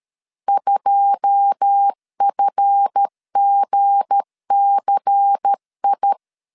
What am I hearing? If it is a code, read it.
Morse code, heard as a keyed tone, about 13 wpm, 800 Hz. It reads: 2FGCI
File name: LEZIONI / 9 2 12wpm 2301